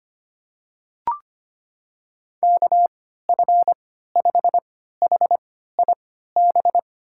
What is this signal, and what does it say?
Morse code, heard as a keyed tone, about 25 wpm, spaced as for 14 wpm, 690 Hz. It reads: KF5HIB
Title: USA Call Signs - 6 Characters - 3x Character Spacing - 25wpm